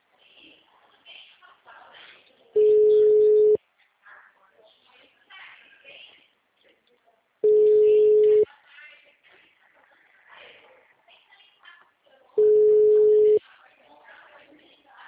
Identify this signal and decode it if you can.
Ringback tone (Europe, 425 Hz), with speech.